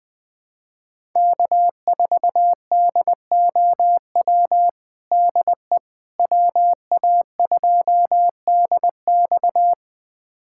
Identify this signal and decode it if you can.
Morse code, heard as a keyed tone, about 21 wpm, 700 Hz. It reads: K4DOW DE WA2DX
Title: USA Callsign DE Callsign - 5 Characters - 20wpm